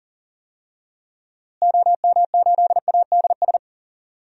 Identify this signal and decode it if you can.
Morse code, heard as a keyed tone, about 37 wpm, 700 Hz. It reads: OM8ADS
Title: DXCC-100 Call Signs - 6 Characters - Repeated 3x - 40wpm